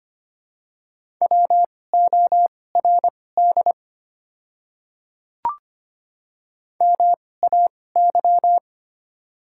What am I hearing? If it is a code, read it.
Morse code, heard as a keyed tone, about 25 wpm, spaced as for 18 wpm, 690 Hz. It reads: WORD MAY
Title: 2x Character Spacing - Top 200 Words - 25wpm